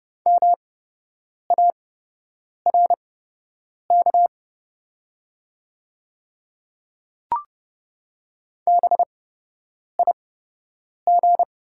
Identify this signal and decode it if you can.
Morse code, heard as a keyed tone, about 30 wpm, spaced as for 8 wpm, 700 Hz. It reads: MARK BIG
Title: 8x Character Spacing - Top 500 Words - 30wpm